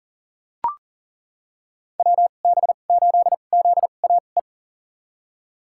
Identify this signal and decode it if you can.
Morse code, heard as a keyed tone, about 40 wpm, spaced as for 29 wpm, 700 Hz. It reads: WB8ZAE
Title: USA Call Signs - 6 Characters - 2x Character Spacing - 40wpm